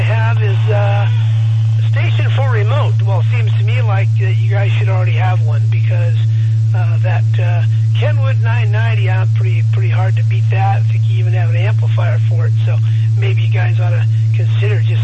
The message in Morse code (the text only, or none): none